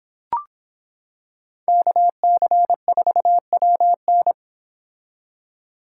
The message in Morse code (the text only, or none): KC4WN